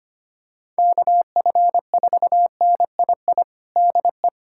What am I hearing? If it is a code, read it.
Morse code, heard as a keyed tone, about 25 wpm, 700 Hz. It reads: KF4NII DE
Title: USA Callsign DE Callsign - 6 Characters - 25wpm